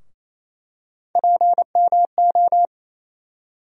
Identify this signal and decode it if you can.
Morse code, heard as a keyed tone, about 28 wpm, 700 Hz. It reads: PMO
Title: USA Call Sign Postfixes - 3 Letters 28wpm